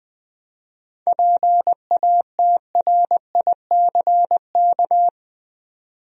Morse code, heard as a keyed tone, about 20 wpm, 700 Hz. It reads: PATRICK
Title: USA Common Names - 20wpm